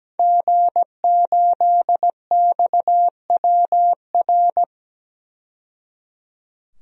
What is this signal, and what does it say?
Morse code, heard as a keyed tone, about 17 wpm, 700 Hz. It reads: G8XWR
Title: DXCC-100 Call Signs - 5 Characters - 17wpm